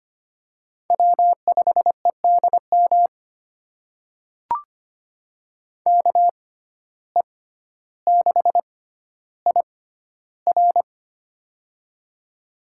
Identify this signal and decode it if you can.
Morse code, heard as a keyed tone, about 25 wpm, spaced as for 13 wpm, 700 Hz. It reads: W5EDM KE6IR
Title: USA Call Signs - 5 Characters - 6x Character Spacing - 25wpm